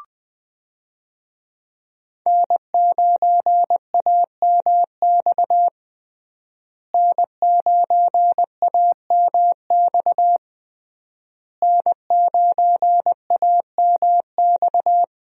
Morse code, heard as a keyed tone, about 20 wpm, 700 Hz. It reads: N9AMX N9AMX N9AMX
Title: USA Call Signs - 5 Characters - Repeated 3x - 20wpm